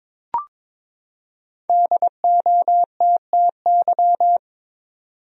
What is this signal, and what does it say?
Morse code, heard as a keyed tone, about 23 wpm, 700 Hz. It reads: DOTTY
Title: USA Common Names - Max 5 Letters - 22wpm